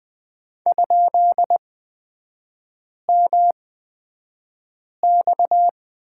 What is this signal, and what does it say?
Morse code, heard as a keyed tone, about 20 wpm, spaced as for 5 wpm, 700 Hz. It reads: ?MX
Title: 236 Lesson 38e - Letter q - Three Characters 20wpm